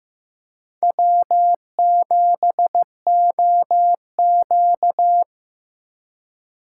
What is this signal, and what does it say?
Morse code, heard as a keyed tone, about 15 wpm, 700 Hz. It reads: W7OQ
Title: K1USN SST QSO Practice - 15wpm